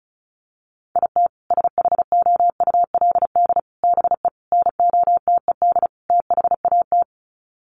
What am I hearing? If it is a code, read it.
Morse code, heard as a keyed tone, about 34 wpm, 700 Hz. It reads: IT SHOULD BE NOTED THAT
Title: Sets of 5 Words 35wpm